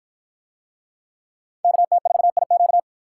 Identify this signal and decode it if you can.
Morse code, heard as a keyed tone, about 50 wpm, 700 Hz. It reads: KT4IQ